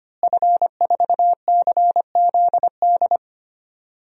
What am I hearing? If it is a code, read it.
Morse code, heard as a keyed tone, about 25 wpm, 700 Hz. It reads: F4CZD
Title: DXCC-100 Call Signs - 5 Characters - Repeated 2x - 25wpm